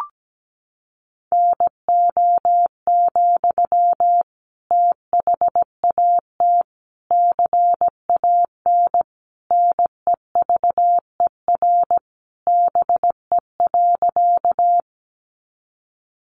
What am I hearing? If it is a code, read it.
Morse code, heard as a keyed tone, about 17 wpm, 700 Hz. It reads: NO, THAT CAN NEVER BE.
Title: Sentences from Top 300 Words 17wpm